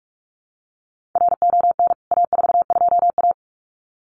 Morse code, heard as a keyed tone, about 44 wpm, 700 Hz. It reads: RON A42A
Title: CWT Exchange 45wpm